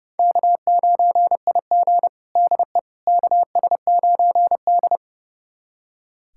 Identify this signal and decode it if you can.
Morse code, heard as a keyed tone, about 31 wpm, 700 Hz. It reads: K9IG DE KS9D
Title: USA Callsign DE Callsign - 4 Characters - 30wpm